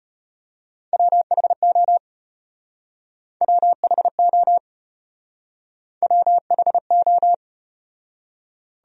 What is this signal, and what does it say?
Morse code, heard as a keyed tone, about 35 wpm, 700 Hz. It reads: WHO WHO WHO